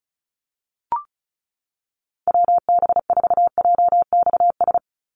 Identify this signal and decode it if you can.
Morse code, heard as a keyed tone, about 36 wpm, 700 Hz. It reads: WB4JXS
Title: USA Call Signs 35wpm